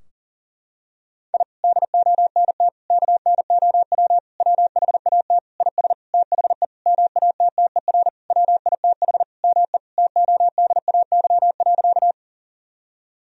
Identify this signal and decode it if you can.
Morse code, heard as a keyed tone, about 40 wpm, 700 Hz. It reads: I DONT KNOW WHAT IS THE MATTER WITH ME TODAY.
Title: Sentences from Top 1000 Words - Review Part 4 - 40wpm